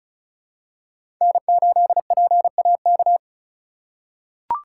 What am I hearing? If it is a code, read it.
Morse code, heard as a keyed tone, about 34 wpm, 700 Hz. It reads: N8PAK
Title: USA Call Signs - 5 Characters - 3x Character Spacing - 35wpm